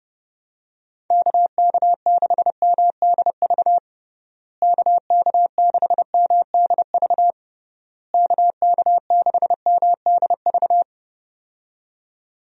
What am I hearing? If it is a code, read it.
Morse code, heard as a keyed tone, about 30 wpm, 700 Hz. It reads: KK6MDV KK6MDV KK6MDV
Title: USA Call Signs - 6 Characters - Repeated 3x - 30wpm